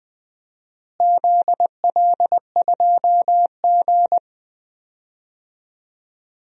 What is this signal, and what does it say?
Morse code, heard as a keyed tone, about 20 wpm, 700 Hz. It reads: ZL2G
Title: DXCC-100 Call Signs - 4 Characters - Repeated 3x - 20wpm